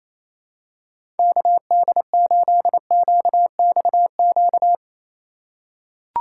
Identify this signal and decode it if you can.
Morse code, heard as a keyed tone, about 28 wpm, 700 Hz. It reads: KD8QXQ
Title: USA Call Signs - 6 Characters - 2x Character Spacing - 28wpm